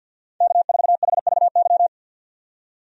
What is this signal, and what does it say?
Morse code, heard as a keyed tone, about 52 wpm, 700 Hz. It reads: K4HVY
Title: USA Call Signs - 5 Characters - 50wpm